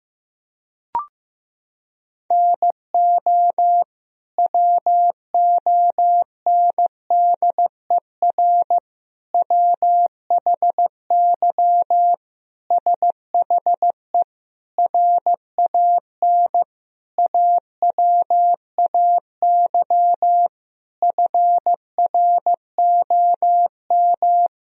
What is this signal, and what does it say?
Morse code, heard as a keyed tone, about 15 wpm, 700 Hz. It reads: NO WONDER WHY SHE RAN AWAY FROM
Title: Sentences from Top 1000 Words - Review Part 3 - 15wpm